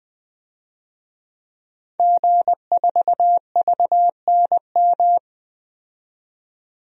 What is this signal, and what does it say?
Morse code, heard as a keyed tone, about 20 wpm, 700 Hz. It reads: G4VNM
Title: DXCC-100 Call Signs - 5 Characters - 20wpm